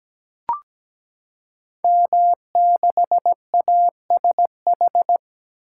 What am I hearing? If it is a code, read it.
Morse code, heard as a keyed tone, about 17 wpm, 700 Hz. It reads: M6ASH
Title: Code Groups - 5 Letter:Numbers - 17wpm